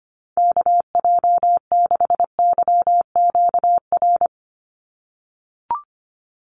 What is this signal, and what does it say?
Morse code, heard as a keyed tone, about 25 wpm, 700 Hz. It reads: KJ6YQR
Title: USA Call Signs 25wpm